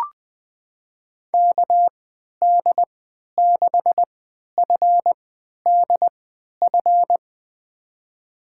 Morse code, heard as a keyed tone, about 20 wpm, spaced as for 11 wpm, 700 Hz. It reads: KD6FDF
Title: USA Call Signs - 6 Characters - 3x Character Spacing - 20wpm